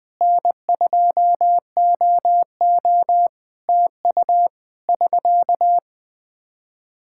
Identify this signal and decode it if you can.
Morse code, heard as a keyed tone, about 20 wpm, 700 Hz. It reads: N2OO TU <SK>
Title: Practice QSO - Exchange 4 - Ending - 20wpm